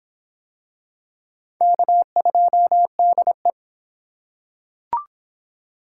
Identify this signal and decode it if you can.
Morse code, heard as a keyed tone, about 22 wpm, 700 Hz. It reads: K2DE